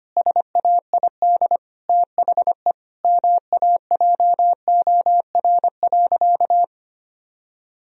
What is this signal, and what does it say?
Morse code, heard as a keyed tone, about 25 wpm, 700 Hz. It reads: SAID THE MAJOR.
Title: Sentences from Top 1000 Words - Review Part 4 - 25wpm